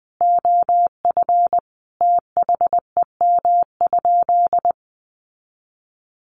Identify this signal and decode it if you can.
Morse code, heard as a keyed tone, about 20 wpm, 700 Hz. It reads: OF THEM?
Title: Sentences from Top 600 Words 20wpm